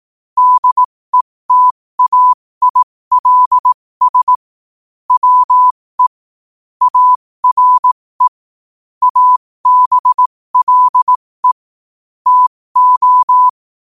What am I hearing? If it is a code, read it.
Morse code, heard as a keyed tone, about 18 wpm, spaced as for 15 wpm, 1000 Hz. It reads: DETAILS WE ARE ABLE TO
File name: qotdfiles / 15wpm